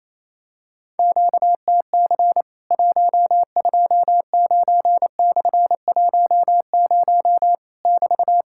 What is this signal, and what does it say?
Morse code, heard as a keyed tone, about 28 wpm, 700 Hz. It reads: QTC 129/10 <BT>
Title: Worked All Europe DX Contest - QTC Exchange - 28wpm